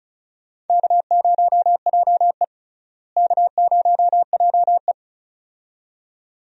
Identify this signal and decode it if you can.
Morse code, heard as a keyed tone, about 36 wpm, 700 Hz. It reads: K0JE K0JE
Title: USA Call Signs - 4 Characters - Repeated 2x - 35wpm